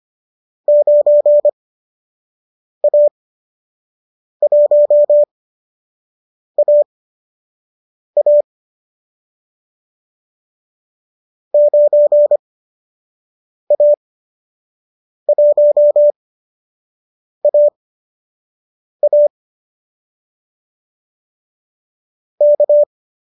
Morse code, heard as a keyed tone, about 25 wpm, spaced as for 6 wpm, 590 Hz. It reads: 9A1AA 9A1AA K
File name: qso012 06